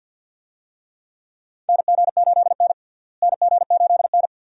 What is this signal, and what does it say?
Morse code, heard as a keyed tone, about 46 wpm, 700 Hz. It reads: NG8N NG8N
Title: USA Call Signs - 4 Characters - Repeated 2x - 50wpm